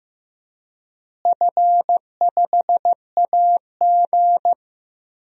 Morse code, heard as a keyed tone, about 16 wpm, 700 Hz. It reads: F5AG
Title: DXCC-100 Call Signs - 4 Characters - Repeated 3x - 15wpm